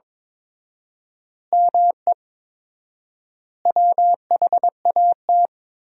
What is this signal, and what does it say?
Morse code, heard as a keyed tone, about 22 wpm, 700 Hz. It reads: ME WHAT
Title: 4x Word-Spacing - Sentences from Top 200 Words - 22wpm